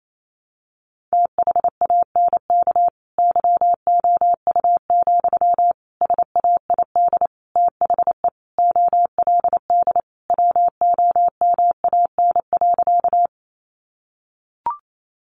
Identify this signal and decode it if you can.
Morse code, heard as a keyed tone, about 28 wpm, 700 Hz. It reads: THANK YOU, SAID THE OLD WOMAN.
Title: Sentences from Top 1100 Words 28wpm